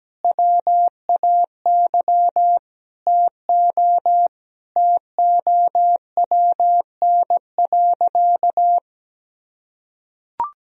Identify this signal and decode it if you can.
Morse code, heard as a keyed tone, about 17 wpm, 700 Hz. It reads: WAY TO TOWN.
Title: Sentences from Top 1000 Words - Review Part 1 - 17wpm